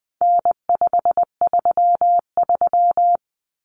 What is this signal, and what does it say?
Morse code, heard as a keyed tone, about 20 wpm, 700 Hz. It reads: N533
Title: Common Cut Numbers - 4-Digit 20wpm